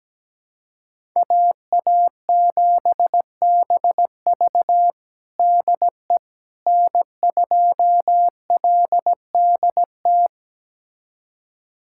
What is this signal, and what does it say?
Morse code, heard as a keyed tone, about 17 wpm, 700 Hz. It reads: AA7BV DE N2LDT
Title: USA Callsign DE Callsign - 5 Characters - 17wpm